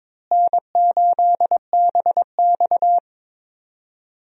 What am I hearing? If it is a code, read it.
Morse code, heard as a keyed tone, about 22 wpm, 700 Hz. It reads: N8BX
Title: USA Call Signs - 4 Characters - Repeated 3x - 22wpm